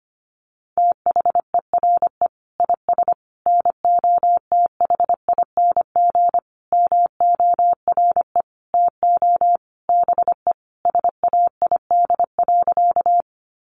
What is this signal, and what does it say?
Morse code, heard as a keyed tone, about 25 wpm, 700 Hz. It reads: THERE IS NOTHING MORE TO BE SAID.
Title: Sentences from Top 500 Words 25wpm